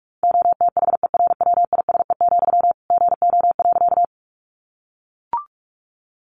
Morse code, heard as a keyed tone, about 44 wpm, 700 Hz. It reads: OTHERWISE, GO.
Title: Sentences from Top 1200 Words 45wpm